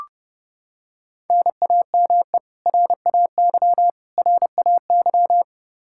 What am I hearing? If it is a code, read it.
Morse code, heard as a keyed tone, about 30 wpm, 700 Hz. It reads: NAME RAY RAY
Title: Practice QSO - Exchange 1 - Beginning - 30wpm